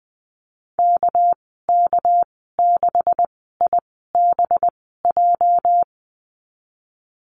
Morse code, heard as a keyed tone, about 20 wpm, 700 Hz. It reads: K K 6 I B J
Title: 238 Lesson 38g - Letter q - Call Signs 20wpm